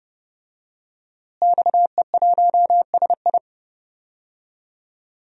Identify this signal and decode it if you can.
Morse code, heard as a keyed tone, about 30 wpm, 700 Hz. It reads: XE1SI
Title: DXCC-100 Call Signs - 5 Characters - Repeated 3x - 30wpm